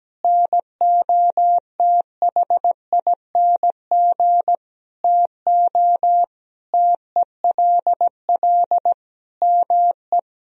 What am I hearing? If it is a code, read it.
Morse code, heard as a keyed tone, about 17 wpm, 700 Hz. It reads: NOTHING TO TELL ME
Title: Sentences from Top 1000 Words - Review Part 4 - 17wpm